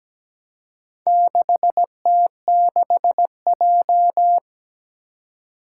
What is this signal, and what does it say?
Morse code, heard as a keyed tone, about 17 wpm, 700 Hz. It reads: 6T6J
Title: Four Challenging Letter-Number Combinations 17wpm